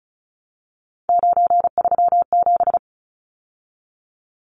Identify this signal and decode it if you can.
Morse code, heard as a keyed tone, about 34 wpm, 700 Hz. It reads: 937
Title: Numbers - 3-Digit 35wpm